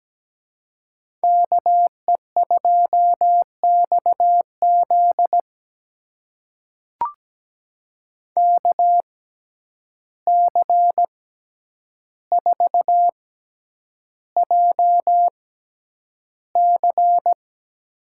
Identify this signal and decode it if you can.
Morse code, heard as a keyed tone, about 17 wpm, spaced as for 9 wpm, 700 Hz. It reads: KE2XZ KC4JC
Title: USA Call Signs - 6x Character Spacing - 17wpm